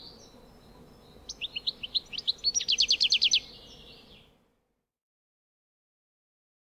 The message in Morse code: none